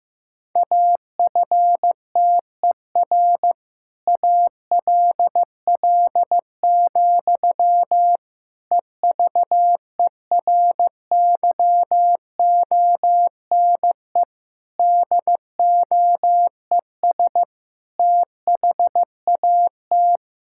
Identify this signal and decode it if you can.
Morse code, heard as a keyed tone, about 15 wpm, 700 Hz. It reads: AFTER ALL, EVERYONE DOES THAT
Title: Sentences from Top 1000 Words - Review Part 5 - 15wpm